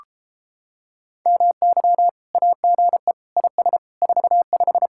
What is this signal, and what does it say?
Morse code, heard as a keyed tone, about 33 wpm, 700 Hz. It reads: MY AGE IS 45